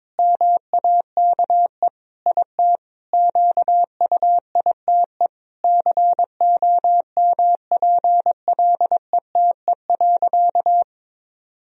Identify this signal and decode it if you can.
Morse code, heard as a keyed tone, about 22 wpm, 700 Hz. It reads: MAKE IT QUITE COMPLETE.
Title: Sentences from Top 1000 Words - Review Part 2 - 22wpm